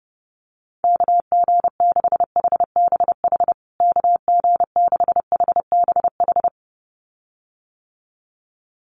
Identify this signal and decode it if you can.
Morse code, heard as a keyed tone, about 30 wpm, 700 Hz. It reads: KG6HBH KG6HBH
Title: USA Call Signs Repeated 2x 30wpm